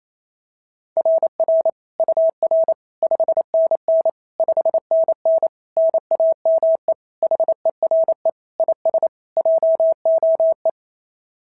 Morse code, heard as a keyed tone, about 25 wpm, 650 Hz. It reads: RR UR 5NN 5NN NAME HERE IS JOE
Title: SOTA QSO Practice 28wpm